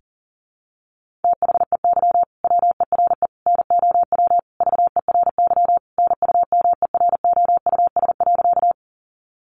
Morse code, heard as a keyed tone, about 40 wpm, 700 Hz. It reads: THEY WERE NOW VERY NUMEROUS.
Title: Sentences from Top 1400 Words 40wpm